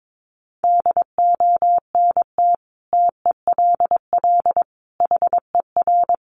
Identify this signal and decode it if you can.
Morse code, heard as a keyed tone, about 22 wpm, 700 Hz. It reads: DONT TELL HER
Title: Sentences from Top 700 Words 22wpm